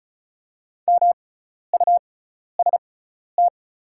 Morse code, heard as a keyed tone, about 36 wpm, spaced as for 12 wpm, 700 Hz. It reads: MUST
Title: 6x Character Spacing - Top 200 Words - 35wpm